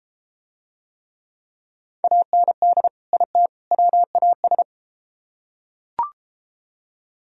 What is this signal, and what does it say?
Morse code, heard as a keyed tone, about 31 wpm, 700 Hz. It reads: AND IT WAS